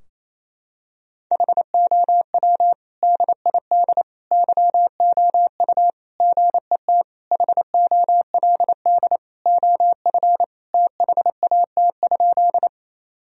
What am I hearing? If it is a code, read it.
Morse code, heard as a keyed tone, about 28 wpm, 700 Hz. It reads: HOW DID YOU GET HOLD OF THAT?
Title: Sentences from Top 1000 Words - Review Part 2 - 28wpm